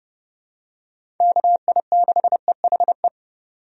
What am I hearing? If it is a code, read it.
Morse code, heard as a keyed tone, about 30 wpm, 700 Hz. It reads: KI6EHE
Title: USA Call Signs - 6 Characters - 30wpm